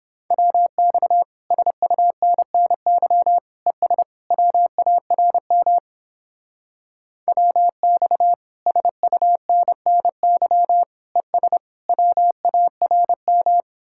Morse code, heard as a keyed tone, about 31 wpm, 700 Hz. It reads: WX SUNNY ES WARM WX SUNNY ES WARM